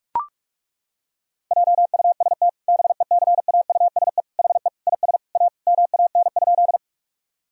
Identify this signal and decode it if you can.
Morse code, heard as a keyed tone, about 41 wpm, 700 Hz. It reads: JUST BECAUSE HE IS A MAN?
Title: Sentences from Top 1000 Words - Review Part 3 - 45wpm